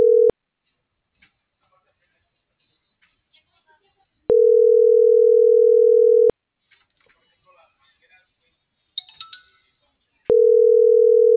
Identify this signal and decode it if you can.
Ringback tone (North America), 440 + 480 Hz.